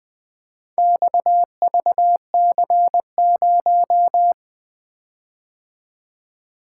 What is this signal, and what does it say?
Morse code, heard as a keyed tone, about 20 wpm, 700 Hz. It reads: XVC0
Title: Four Letter-Number Combinations - 20wpm